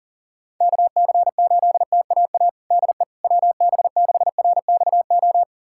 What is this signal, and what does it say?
Morse code, heard as a keyed tone, about 40 wpm, 700 Hz. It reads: KC8TAA DE WB6RXO
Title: USA Callsign DE Callsign - 6 Characters - 40wpm